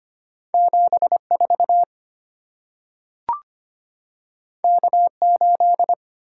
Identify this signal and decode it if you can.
Morse code, heard as a keyed tone, about 25 wpm, 700 Hz. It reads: 7 4 K 8
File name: Two Letter-Number Combinations - 25wpm